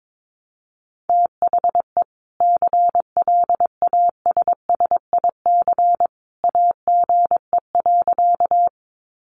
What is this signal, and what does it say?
Morse code, heard as a keyed tone, about 22 wpm, 700 Hz. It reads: THE CLASSIC AGE.
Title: Sentences from Top 1000 Words 22wpm